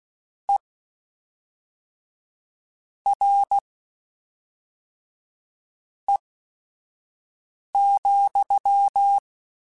Morse code, heard as a keyed tone, about 16 wpm, spaced as for 4 wpm, 800 Hz. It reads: ERE,
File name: General / General-mp3s / Suffixes sp @5